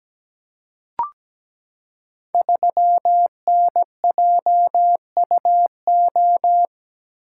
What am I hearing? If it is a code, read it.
Morse code, heard as a keyed tone, about 17 wpm, 700 Hz. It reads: 3NJUO
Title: Code Groups - 5 Letter:Numbers - 17wpm